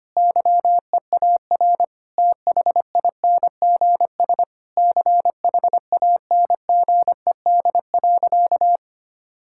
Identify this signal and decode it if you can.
Morse code, heard as a keyed tone, about 25 wpm, 700 Hz. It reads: YEAR THINGS CHANGED.
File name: practice / 25wpm / Sentences with 4-Word Phrases - 25wpm